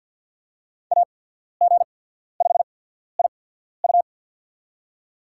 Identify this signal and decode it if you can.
Morse code, heard as a keyed tone, about 53 wpm, spaced as for 14 wpm, 700 Hz. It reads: AG5IU
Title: USA Call Signs - 5 Characters - 8x Character Spacing - 50wpm